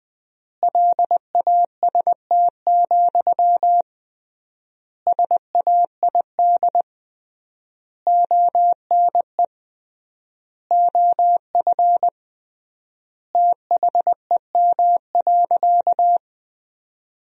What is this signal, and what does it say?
Morse code, heard as a keyed tone, about 20 wpm, 700 Hz. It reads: LAST, SAID ONE OF THEM.
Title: 3x Word-Spacing - Sentences from Top 300 Words - 20wpm